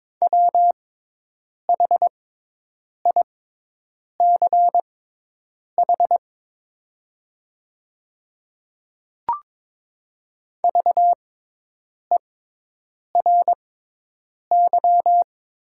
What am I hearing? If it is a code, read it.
Morse code, heard as a keyed tone, about 22 wpm, spaced as for 8 wpm, 700 Hz. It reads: WHICH VERY
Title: 6x Character Spacing - Top 200 Words - 22wpm